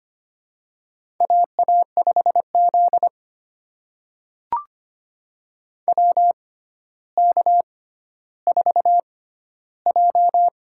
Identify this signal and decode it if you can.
Morse code, heard as a keyed tone, about 25 wpm, spaced as for 13 wpm, 700 Hz. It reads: AA5Z WK4J
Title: USA Call Signs - 4 Characters - 6x Character Spacing - 25wpm